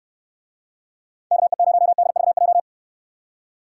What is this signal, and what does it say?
Morse code, heard as a keyed tone, about 68 wpm, 700 Hz. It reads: KE0DVJ